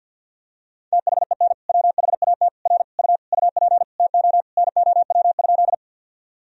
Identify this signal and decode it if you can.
Morse code, heard as a keyed tone, about 50 wpm, 700 Hz. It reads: THEN WHAT R U UP TO NOW?
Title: Sentences from Top 100 Words and Abbreviations 50wpm